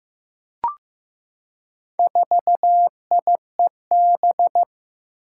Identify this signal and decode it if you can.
Morse code, heard as a keyed tone, about 15 wpm, 700 Hz. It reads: 4IEB
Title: Four Challenging Letter-Number Combinations 15wpm